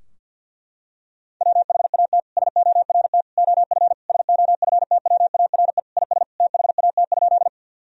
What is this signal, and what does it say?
Morse code, heard as a keyed tone, about 49 wpm, 700 Hz. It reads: WHAT SORT OF SOFTWARE IS THAT?
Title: Sentences from Top 1000 Words - Review Part 3 - 50wpm